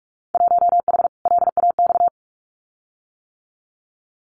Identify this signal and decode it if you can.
Morse code, heard as a keyed tone, about 44 wpm, 700 Hz. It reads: 1H LAX
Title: Winter Field Day Exchanges 45wpm